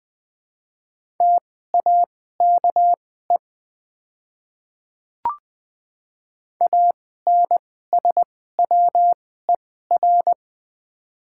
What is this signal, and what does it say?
Morse code, heard as a keyed tone, about 20 wpm, spaced as for 14 wpm, 700 Hz. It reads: TAKE ANSWER
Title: 2x Character Spacing - Top 500 Words - 20wpm